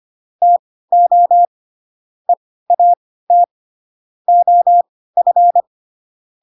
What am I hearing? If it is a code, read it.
Morse code, heard as a keyed tone, about 25 wpm, spaced as for 16 wpm, 700 Hz. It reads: TO EAT OF
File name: Pinocchio / Pinocchio ch7 10 25 ik1hge